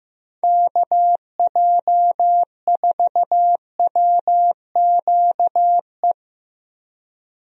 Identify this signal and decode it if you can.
Morse code, heard as a keyed tone, about 15 wpm, 700 Hz. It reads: KJ4WQE
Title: USA Call Signs - 6 Characters - Repeated 2x - 15wpm